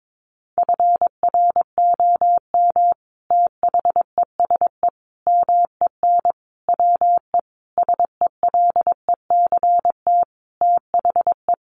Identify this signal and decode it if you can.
Morse code, heard as a keyed tone, about 22 wpm, 700 Hz. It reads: FROM THESE MEN WE SELECT THE
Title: Sentences from Top 1300 Words 22wpm